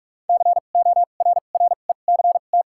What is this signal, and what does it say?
Morse code, heard as a keyed tone, about 43 wpm, spaced as for 32 wpm, 700 Hz. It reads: CORRECT
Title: 2x Character Spacing - Top 500 Words - 45wpm